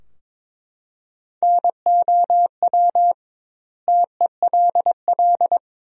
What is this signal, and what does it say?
Morse code, heard as a keyed tone, about 21 wpm, 700 Hz. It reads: NOW TELL
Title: 2x Word-Spacing - Sentences from Top 200 Words - 22wpm